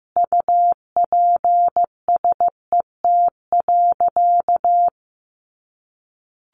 Morse code, heard as a keyed tone, about 14 wpm, 700 Hz. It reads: UPSET.
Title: Sentences from Top 1200 Words 15wpm